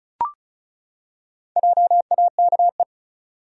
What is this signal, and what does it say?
Morse code, heard as a keyed tone, about 34 wpm, 700 Hz. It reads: JAKE
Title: USA Common Names - Max 4 Letters - 35wpm